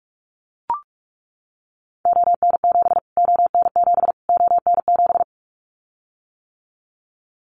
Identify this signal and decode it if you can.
Morse code, heard as a keyed tone, about 44 wpm, 700 Hz. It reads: ON7 ON7 ON7
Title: Callsign Prefixes for Top 25 DXCC - Repeated 3x - 45wpm